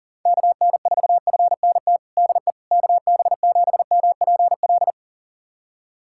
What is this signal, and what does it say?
Morse code, heard as a keyed tone, about 37 wpm, 700 Hz. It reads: KN4FNT DE KB7MPL
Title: USA Callsign DE Callsign - 6 Characters - 40wpm